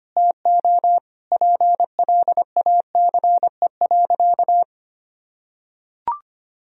Morse code, heard as a keyed tone, about 25 wpm, 700 Hz. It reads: TO PLACE.
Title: Sentences from Top 1000 Words - Review Part 1 - 25wpm